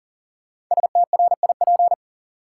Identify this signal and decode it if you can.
Morse code, heard as a keyed tone, about 40 wpm, 700 Hz. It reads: STRIP
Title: Sentences from Top 1600 Words - New Word Drill Alt - 40wpm